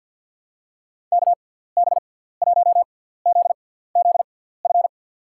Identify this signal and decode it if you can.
Morse code, heard as a keyed tone, about 52 wpm, spaced as for 17 wpm, 700 Hz. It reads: KB1ZZF
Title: USA Call Signs - 6 Characters - 6x Character Spacing - 50wpm